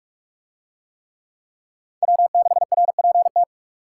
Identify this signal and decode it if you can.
Morse code, heard as a keyed tone, about 44 wpm, 700 Hz. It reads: W6RPT